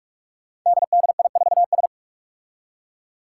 Morse code, heard as a keyed tone, about 44 wpm, 700 Hz. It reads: DDI4S
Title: Five Challenging Letter-Number Combinations 45wpm